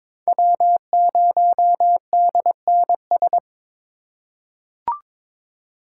Morse code, heard as a keyed tone, about 22 wpm, 700 Hz. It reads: W0DNS
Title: USA Call Signs - 8x Character Spacing - 22wpm